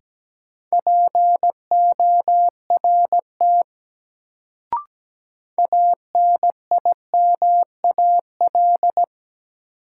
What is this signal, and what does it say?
Morse code, heard as a keyed tone, about 17 wpm, 700 Hz. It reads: PORT ANIMAL
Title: Sentences from Top 1000 Words - Review Part 1 - New Word Drill - 17wpm